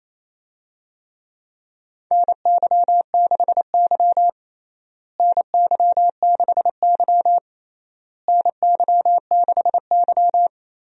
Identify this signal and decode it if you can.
Morse code, heard as a keyed tone, about 28 wpm, 700 Hz. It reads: NY6Y NY6Y NY6Y
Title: USA Call Signs - 4 Characters - Repeated 3x - 28wpm